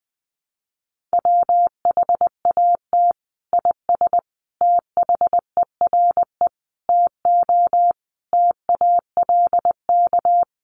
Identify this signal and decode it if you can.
Morse code, heard as a keyed tone, about 20 wpm, 700 Hz. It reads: WHAT IS THERE TO TALK
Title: Sentences from Top 400 Words 20wpm